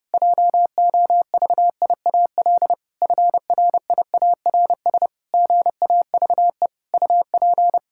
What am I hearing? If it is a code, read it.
Morse code, heard as a keyed tone, about 30 wpm, 700 Hz. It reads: JOVIAL FRIARS GAVE UP
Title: Pangram Challenge v2 - 30wpm